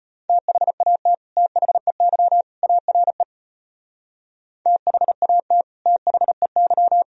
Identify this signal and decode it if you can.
Morse code, heard as a keyed tone, about 31 wpm, 700 Hz. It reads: THAT THEY ARE THAT THEY
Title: Speed-Racing - Sets of 3 Words 25wpm